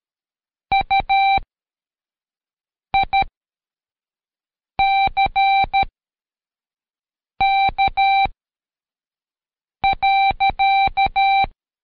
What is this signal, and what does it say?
Morse code, heard as a keyed tone, about 13 wpm, spaced as for 5 wpm, 780 Hz. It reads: UICK.
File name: Code Quick / AudioDownld2 / Seg102-Day12-FinalDrill